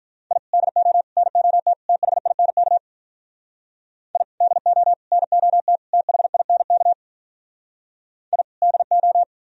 Morse code, heard as a keyed tone, about 53 wpm, 700 Hz. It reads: I DO NOT THINK I DO NOT THINK I DO